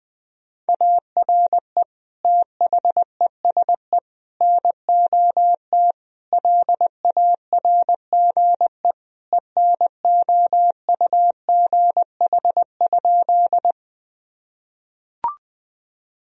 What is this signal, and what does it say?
Morse code, heard as a keyed tone, about 20 wpm, 700 Hz. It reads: ARE THESE NOT LARGE ENOUGH?
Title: Sentences from Top 1000 Words - Review Part 1 - 20wpm